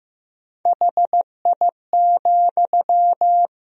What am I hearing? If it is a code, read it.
Morse code, heard as a keyed tone, about 15 wpm, 700 Hz. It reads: HI,
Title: Sentences from Top 1000 Words - Review Part 3 - 15wpm